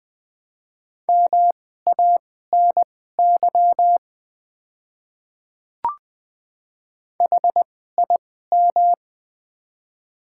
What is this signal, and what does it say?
Morse code, heard as a keyed tone, about 20 wpm, spaced as for 14 wpm, 700 Hz. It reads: MANY HIM
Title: 2x Character Spacing - Top 300 Words - 20wpm